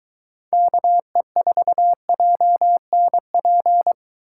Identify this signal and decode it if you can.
Morse code, heard as a keyed tone, about 23 wpm, 700 Hz. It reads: KE4JNP